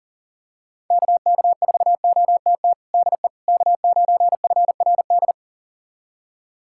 Morse code, heard as a keyed tone, about 40 wpm, 700 Hz. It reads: KK4OTT DE K9FRD
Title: USA Callsign DE Callsign 40wpm